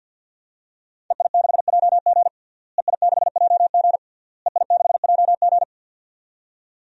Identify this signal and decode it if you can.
Morse code, heard as a keyed tone, about 52 wpm, 700 Hz. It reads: EI6JG EI6JG EI6JG
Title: DXCC-100 Call Signs - 5 Characters - Repeated 3x - 50wpm